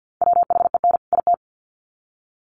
Morse code, heard as a keyed tone, about 50 wpm, 700 Hz. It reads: WHEN IT
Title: Sets of 2 Words 50wpm